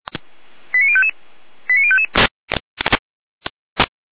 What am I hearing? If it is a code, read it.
Five-tone selective call (ZVEI), heard as quick runs of five tones, tones of about 70 ms. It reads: 89055 89055